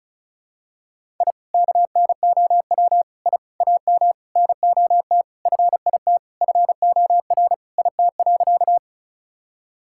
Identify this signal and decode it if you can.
Morse code, heard as a keyed tone, about 36 wpm, 700 Hz. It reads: I KNOW I AM NOT FIT FOR IT.
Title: Sentences from Top 1000 Words - Review Part 5 - 35wpm